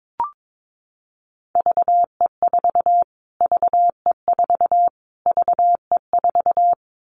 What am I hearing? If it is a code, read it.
Morse code, heard as a keyed tone, about 22 wpm, 700 Hz. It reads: VE4 VE4 VE4
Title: Callsign Prefixes for Top 25 DXCC - Repeated 3x - 22wpm